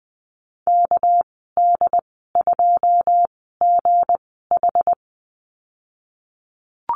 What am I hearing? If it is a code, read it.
Morse code, heard as a keyed tone, about 20 wpm, spaced as for 15 wpm, 700 Hz. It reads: KD2GH
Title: 200 Lesson 32g - Letter k - Call Signs 20wpm